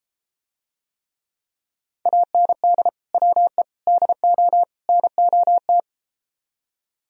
Morse code, heard as a keyed tone, about 33 wpm, 700 Hz. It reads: AND WE DO NOT